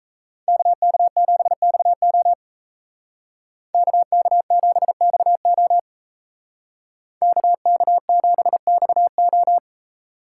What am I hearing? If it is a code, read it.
Morse code, heard as a keyed tone, about 37 wpm, 700 Hz. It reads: KK7XO KK7XO KK7XO